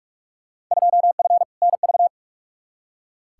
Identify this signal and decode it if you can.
Morse code, heard as a keyed tone, about 46 wpm, 700 Hz. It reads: 2F NV
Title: Field Day Exchanges - 45wpm